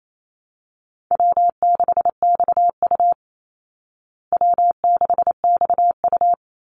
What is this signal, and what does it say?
Morse code, heard as a keyed tone, about 28 wpm, 690 Hz. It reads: W6XU W6XU
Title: Call Signs Repeated 3x - 28wpm